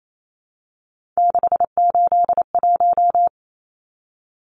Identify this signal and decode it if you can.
Morse code, heard as a keyed tone, about 28 wpm, 700 Hz. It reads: 681
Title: Numbers - 3-Digit 28wpm